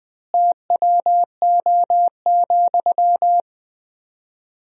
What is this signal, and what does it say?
Morse code, heard as a keyed tone, about 21 wpm, 700 Hz. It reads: TWO,
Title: 4x Word-Spacing - Sentences from Top 200 Words - 20wpm